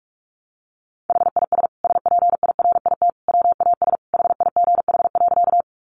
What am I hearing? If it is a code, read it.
Morse code, heard as a keyed tone, about 44 wpm, 700 Hz. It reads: HIS SPIRIT WAS HIGH.
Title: Sentences from Top 1400 Words 45wpm